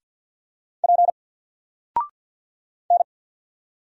Morse code, heard as a keyed tone, about 48 wpm, 700 Hz. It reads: P N
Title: Single Character (Letter:Number:Punctuation:Prosign) 50wpm